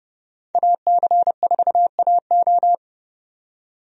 Morse code, heard as a keyed tone, about 32 wpm, 700 Hz. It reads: AC4AO
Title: USA Call Signs - 5 Characters - 30wpm